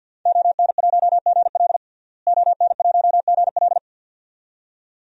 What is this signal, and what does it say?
Morse code, heard as a keyed tone, about 46 wpm, 700 Hz. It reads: ON1GL ON1GL
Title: DXCC-100 Call Signs - 5 Characters - Repeated 2x - 50wpm